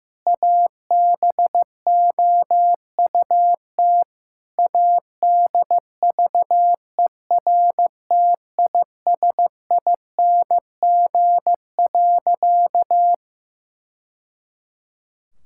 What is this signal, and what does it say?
Morse code, heard as a keyed tone, about 15 wpm, 700 Hz. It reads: ABOUT ADVERTISING.
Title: Sentences from Top 1000 Words - Review Part 6 - 15wpm